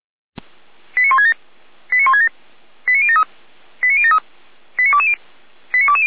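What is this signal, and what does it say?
Five-tone selective call (ZVEI), heard as quick runs of five tones, tones of about 80 ms. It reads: 89167 89167 89073 89073 89220 89220
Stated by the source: ZVEI five-tone selective call received by SDR, Feuerwehr Orlamünde.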